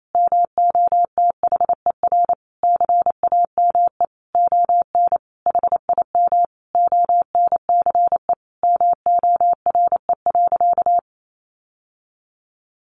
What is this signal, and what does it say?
Morse code, heard as a keyed tone, about 28 wpm, 700 Hz. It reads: MOTHER CAME ON HIM ONCE MORE.
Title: Sentences from Top 1100 Words 28wpm